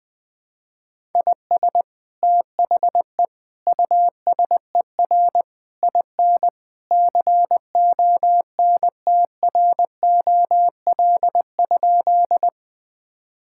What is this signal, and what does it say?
Morse code, heard as a keyed tone, about 20 wpm, 700 Hz. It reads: IS THE USER IN CONTROL?
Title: Sentences from Top 1000 Words - Review Part 5 - 20wpm